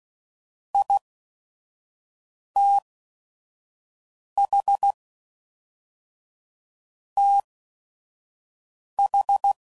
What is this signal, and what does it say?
Morse code, heard as a keyed tone, about 16 wpm, spaced as for 5 wpm, 800 Hz. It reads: ITH TH
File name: CW Code Groups @5WPM / Technical Subjects / OR Gate